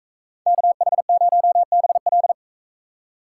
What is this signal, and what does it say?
Morse code, heard as a keyed tone, about 38 wpm, 700 Hz. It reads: KH0BL